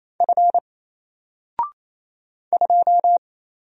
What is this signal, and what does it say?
Morse code, heard as a keyed tone, about 27 wpm, 700 Hz. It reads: F 2
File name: Single-Letter-Number-28wpm